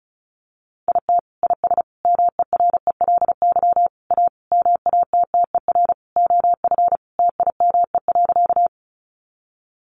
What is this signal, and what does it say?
Morse code, heard as a keyed tone, about 34 wpm, 700 Hz. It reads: IT IS MERELY A MATTER OF TIME.
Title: Sentences from Top 1400 Words 35wpm